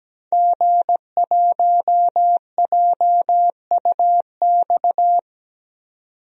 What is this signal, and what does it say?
Morse code, heard as a keyed tone, about 17 wpm, 700 Hz. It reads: G1JUX
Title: DXCC-100 Call Signs - 5 Characters - 17wpm